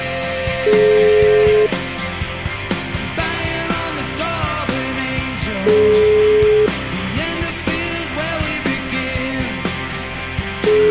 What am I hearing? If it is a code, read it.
Ringback tone (Europe, 425 Hz), with noise.